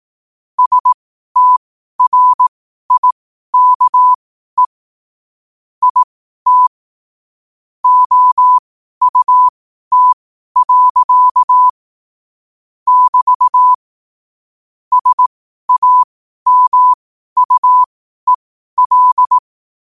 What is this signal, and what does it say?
Morse code, heard as a keyed tone, about 18 wpm, spaced as for 12 wpm, 1000 Hz. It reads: STRIKE IT OUT. <BT> SAMUEL
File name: qotdfiles / 12wpm